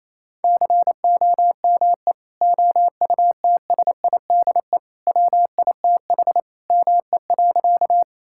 Morse code, heard as a keyed tone, about 28 wpm, 700 Hz. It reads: COME OUTSIDE WITH ME.
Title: Sentences from Top 1000 Words - Review Part 6 - 28wpm